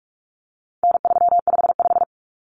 Morse code, heard as a keyed tone, about 44 wpm, 700 Hz. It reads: N355
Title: Common Cut Numbers - 4-Digit 45wpm